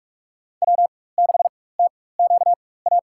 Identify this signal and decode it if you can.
Morse code, heard as a keyed tone, about 43 wpm, spaced as for 21 wpm, 700 Hz. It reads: W6TQA
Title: USA Call Signs - 5 Characters - 4x Character Spacing - 45wpm